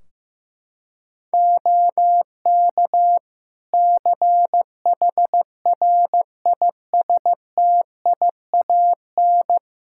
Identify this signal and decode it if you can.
Morse code, heard as a keyed tone, about 15 wpm, 700 Hz. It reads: OK CHRISTIAN
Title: Practice QSO - Exchange 4 - Ending - 15wpm